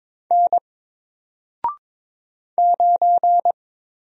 Morse code, heard as a keyed tone, about 22 wpm, 700 Hz. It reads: N 9
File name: Single-Letter-Number-22wpm